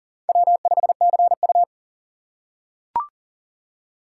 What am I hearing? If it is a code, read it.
Morse code, heard as a keyed tone, about 40 wpm, 700 Hz. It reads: W5CU
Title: USA Call Signs - 2x Character Spacing - 40wpm